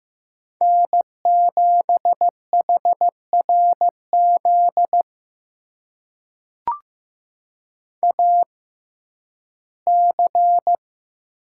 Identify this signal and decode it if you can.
Morse code, heard as a keyed tone, about 15 wpm, spaced as for 11 wpm, 700 Hz. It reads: N7HRZ AC
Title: USA Call Signs - 5 Characters - 6x Character Spacing - 15wpm